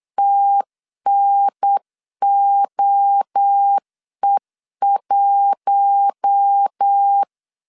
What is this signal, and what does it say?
Morse code, heard as a keyed tone, about 8 wpm, 800 Hz. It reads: TNOE1
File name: CW / DET / 1 6 1500C S C 8wpm 5219